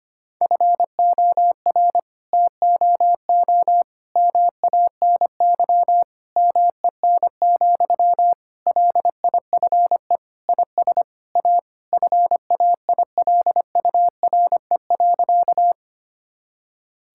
Morse code, heard as a keyed tone, about 25 wpm, 700 Hz. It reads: FOR TOO MANY MEN, LIFE IS A FAILURE.
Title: Sentences from Top 1000 Words - Review Part 6 - 25wpm